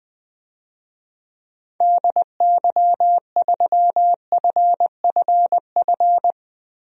Morse code, heard as a keyed tone, about 21 wpm, 700 Hz. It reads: DY3FFF
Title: DXCC-100 Call Signs - 6 Characters - 20wpm